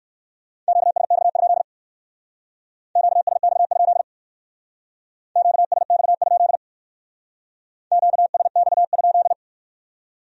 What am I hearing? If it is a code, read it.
Morse code, heard as a keyed tone, about 60 wpm, 700 Hz. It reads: QSX? QSX? QSX? QSX?